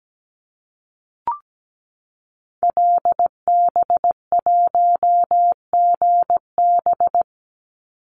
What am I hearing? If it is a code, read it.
Morse code, heard as a keyed tone, about 17 wpm, 700 Hz. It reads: LB1GB
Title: Call Signs Repeated 3x - 17wpm